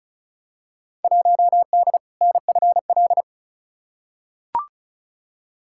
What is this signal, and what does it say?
Morse code, heard as a keyed tone, about 34 wpm, 700 Hz. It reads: 1D NFL
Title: Field Day Exchanges - 35wpm